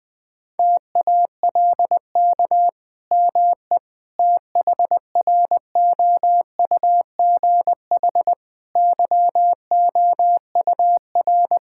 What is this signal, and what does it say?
Morse code, heard as a keyed tone, about 20 wpm, 700 Hz. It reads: TALK ME THROUGH YOUR
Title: Sentences from Top 1000 Words - Review Part 5 - 20wpm